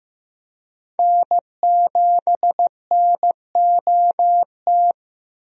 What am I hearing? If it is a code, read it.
Morse code, heard as a keyed tone, about 15 wpm, 700 Hz. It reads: N7NOT
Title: USA Call Signs - 5 Characters - Repeated 3x - 15wpm